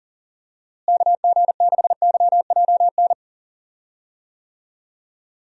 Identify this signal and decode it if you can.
Morse code, heard as a keyed tone, about 40 wpm, 700 Hz. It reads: KG6YJN
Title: USA Call Signs - 6 Characters - Repeated 2x - 40wpm